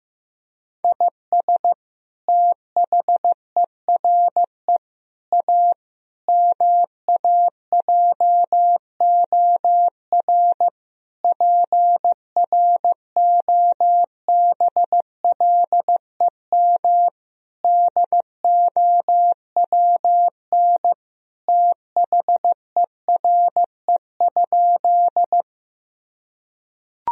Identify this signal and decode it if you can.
Morse code, heard as a keyed tone, about 15 wpm, 700 Hz. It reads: IS THERE A MAJOR PROBLEM DOWN THERE?
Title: Sentences from Top 1000 Words - Review Part 4 - 15wpm